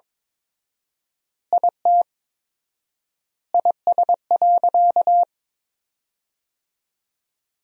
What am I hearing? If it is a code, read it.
Morse code, heard as a keyed tone, about 22 wpm, 700 Hz. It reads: IT IS.
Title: 4x Word-Spacing - Sentences from Top 300 Words - 22wpm